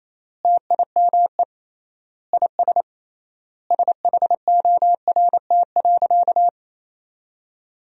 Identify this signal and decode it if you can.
Morse code, heard as a keyed tone, about 28 wpm, 700 Hz. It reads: TIME IS SHORT.
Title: 3x Word-Spacing - Sentences from Top 500 Words - 28wpm